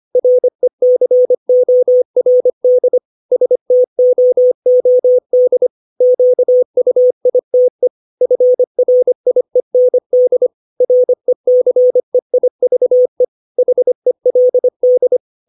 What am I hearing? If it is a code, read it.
Morse code, heard as a keyed tone, about 25 wpm, 500 Hz. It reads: RECORD STOOD QUITE FRIEND RECEIVE HELD
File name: 632 words / 632 Words 25WPM